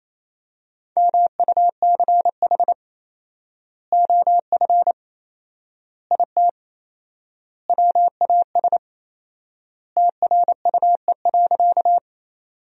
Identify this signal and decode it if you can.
Morse code, heard as a keyed tone, about 28 wpm, 700 Hz. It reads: MUCH OF IT WAS TRUE.
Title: 4x Word-Spacing - Sentences from Top 400 Words - 28wpm